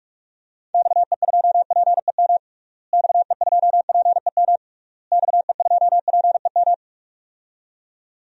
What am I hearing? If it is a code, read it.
Morse code, heard as a keyed tone, about 46 wpm, 700 Hz. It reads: XE2PEM XE2PEM XE2PEM
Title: DXCC-100 Call Signs - 6 Characters - Repeated 3x - 45wpm